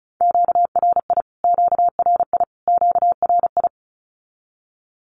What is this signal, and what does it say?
Morse code, heard as a keyed tone, about 34 wpm, 700 Hz. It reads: QRI QRI QRI
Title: Q-Codes Repeated 3x 35wpm